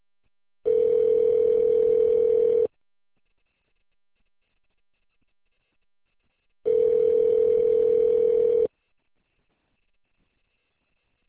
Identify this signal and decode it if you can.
Ringback tone (North America), 440 + 480 Hz.